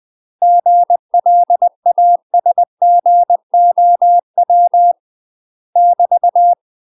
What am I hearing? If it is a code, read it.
Morse code, heard as a keyed tone, about 20 wpm, 700 Hz. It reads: GLASGOW <BT>
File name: UK MASTER QSOs TRAINING / 20WPM / UK MASTER QSOs 0012